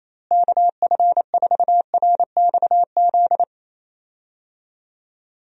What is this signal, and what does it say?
Morse code, heard as a keyed tone, about 28 wpm, 700 Hz. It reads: KF4RXZ
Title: USA Call Signs - 6 Characters - Repeated 3x - 28wpm